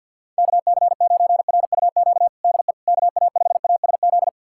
Morse code, heard as a keyed tone, about 53 wpm, 700 Hz. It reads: KC9RUQ DE KA5ASZ
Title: USA Callsign DE Callsign - 6 Characters - 50wpm